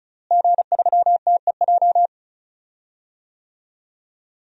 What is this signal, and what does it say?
Morse code, heard as a keyed tone, about 36 wpm, 700 Hz. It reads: G3TEJ
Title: DXCC-100 Call Signs - 5 Characters - Repeated 3x - 35wpm